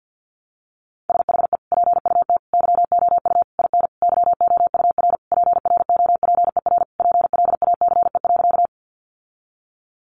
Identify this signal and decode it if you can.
Morse code, heard as a keyed tone, about 50 wpm, 700 Hz. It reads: SHE PUT YOU IN YOUR PROPER PLACE.
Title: Sentences from Top 800 Words 50wpm